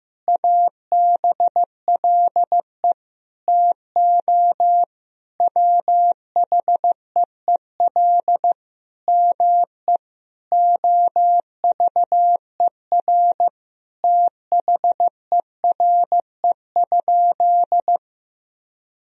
Morse code, heard as a keyed tone, about 15 wpm, 700 Hz. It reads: ABLE TO WHEEL ME OVER THERE?
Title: Sentences from Top 1000 Words - Review Part 3 - 15wpm